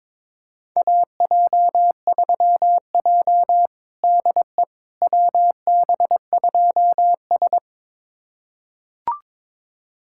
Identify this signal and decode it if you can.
Morse code, heard as a keyed tone, about 22 wpm, 700 Hz. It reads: AJ3J DE WB2S
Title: USA Callsign DE Callsign - 4 Characters - 22wpm